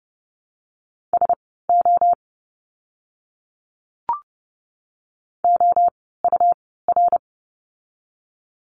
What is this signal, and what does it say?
Morse code, heard as a keyed tone, about 30 wpm, spaced as for 17 wpm, 700 Hz. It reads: SO OUR
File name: course / 30:10wpm / 230 Lesson 37f - Letter x - Words 30wpm